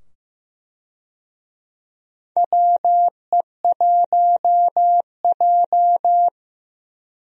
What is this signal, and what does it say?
Morse code, heard as a keyed tone, about 15 wpm, 700 Hz. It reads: WE1J
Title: USA Call Signs - 4 Characters - 15wpm